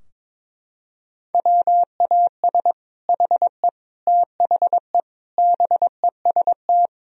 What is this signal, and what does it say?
Morse code, heard as a keyed tone, about 22 wpm, 700 Hz. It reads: WAS HE THE BEST
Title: Sentences from Top 1000 Words - Review Part 4 - 22wpm